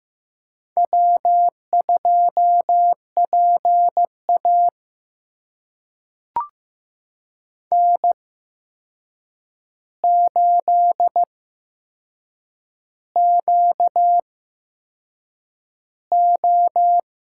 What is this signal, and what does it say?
Morse code, heard as a keyed tone, about 15 wpm, spaced as for 6 wpm, 700 Hz. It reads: W2PA N8QO
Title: USA Call Signs - 4 Characters - 8x Character Spacing - 15wpm